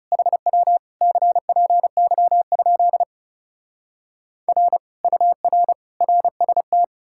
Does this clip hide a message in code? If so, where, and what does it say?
Yes: Morse code from 0.1 to 6.8 s, heard as a keyed tone, about 40 wpm, 700 Hz. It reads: HW CPY? R UR RST